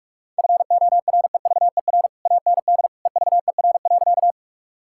Morse code, heard as a keyed tone, about 46 wpm, 700 Hz. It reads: FOREVER AND EVER.
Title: Sentences from Top 1600 Words 45wpm